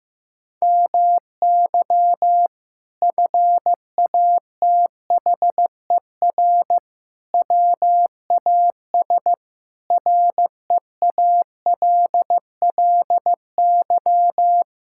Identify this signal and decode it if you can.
Morse code, heard as a keyed tone, about 15 wpm, 700 Hz. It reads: MY FATHER WAS REALLY
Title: Sentences from Top 1000 Words - Review Part 6 - 15wpm